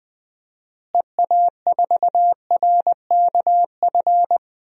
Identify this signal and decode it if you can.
Morse code, heard as a keyed tone, about 20 wpm, 700 Hz. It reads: EA4RKF
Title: DXCC-100 Call Signs - 6 Characters - Repeated 3x - 20wpm